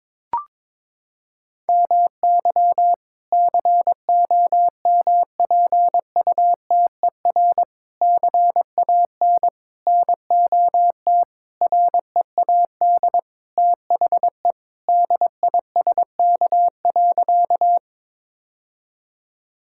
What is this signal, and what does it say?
Morse code, heard as a keyed tone, about 22 wpm, 700 Hz. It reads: MY COMPUTER CAN NOT READ THE DISK.
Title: Sentences from Top 1000 Words - Review Part 6 - 22wpm